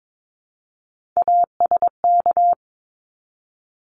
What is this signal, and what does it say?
Morse code, heard as a keyed tone, about 22 wpm, 700 Hz. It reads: ASK
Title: Top 300 Words Repeated 22wpm